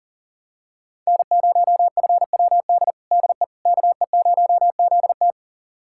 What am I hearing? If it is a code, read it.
Morse code, heard as a keyed tone, about 40 wpm, 700 Hz. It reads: N0FWD DE KE0ZT
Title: USA Callsign DE Callsign - 5 Characters - 40wpm